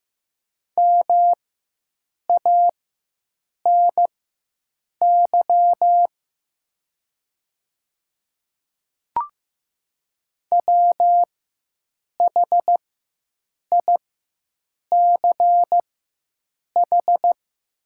Morse code, heard as a keyed tone, about 15 wpm, spaced as for 7 wpm, 700 Hz. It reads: MANY WHICH
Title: 4x Character Spacing - Top 100 Words - 15wpm